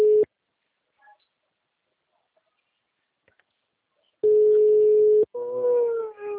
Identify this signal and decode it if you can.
Ringback tone (Europe, 425 Hz), with speech.
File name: out- -2010- - - .1117